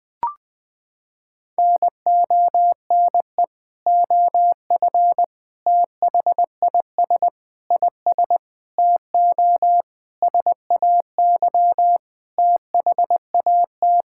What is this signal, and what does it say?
Morse code, heard as a keyed tone, about 20 wpm, 700 Hz. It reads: NONE OF THIS IS TO SAY THAT
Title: Sentences with 7-Word Phrases - 20wpm